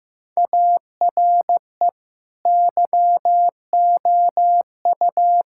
Morse code, heard as a keyed tone, about 15 wpm, 700 Hz. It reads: ARE YOU
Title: Sentences from Top 1600 Words 15wpm